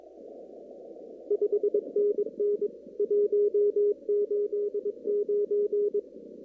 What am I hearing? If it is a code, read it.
Morse code, heard as a keyed tone, about 23 wpm, 410 Hz. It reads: 5NN 189
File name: CW / kt5j wpx cw-op-k5pi